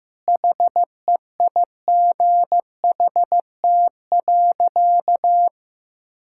Morse code, heard as a keyed tone, about 15 wpm, 700 Hz. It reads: HEIGHT.
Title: Sentences from Top 1600 Words 15wpm